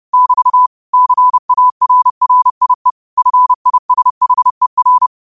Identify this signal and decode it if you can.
Morse code, heard as a keyed tone, about 30 wpm, 1000 Hz. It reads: <BT> CARRIE FISHER
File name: qotdfiles / 30wpm